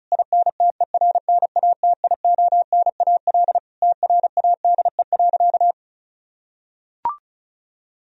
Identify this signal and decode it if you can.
Morse code, heard as a keyed tone, about 34 wpm, 700 Hz. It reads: INTERNATIONAL TRADE.
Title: Sentences from Top 1000 Words - Review Part 4 - 35wpm